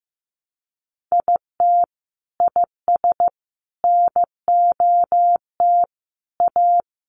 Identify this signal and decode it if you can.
Morse code, heard as a keyed tone, about 15 wpm, 700 Hz. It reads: IT IS NOT A
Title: Sentences from Top 800 Words 15wpm